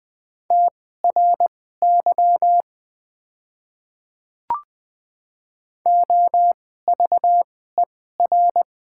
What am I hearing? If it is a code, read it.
Morse code, heard as a keyed tone, about 20 wpm, spaced as for 14 wpm, 700 Hz. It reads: TRY OVER
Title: 2x Character Spacing - Top 300 Words - 20wpm